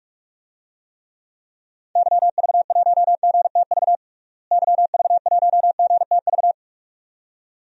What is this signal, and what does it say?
Morse code, heard as a keyed tone, about 42 wpm, 700 Hz. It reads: YV1GTV YV1GTV
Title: DXCC-100 Call Signs - 6 Characters - Repeated 2x - 45wpm